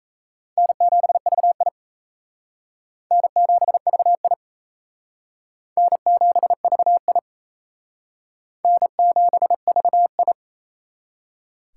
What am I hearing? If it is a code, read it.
Morse code, heard as a keyed tone, about 37 wpm, 700 Hz. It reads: N7VI N7VI N7VI N7VI